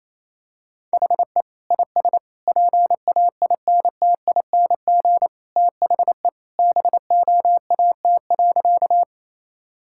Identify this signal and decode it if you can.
Morse code, heard as a keyed tone, about 28 wpm, 700 Hz. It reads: HE IS PAINTING THE BOAT.
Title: Sentences from Top 1000 Words - Review Part 6 - 28wpm